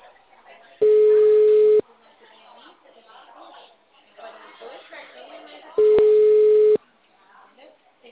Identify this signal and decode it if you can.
Ringback tone (Europe, 425 Hz), with speech.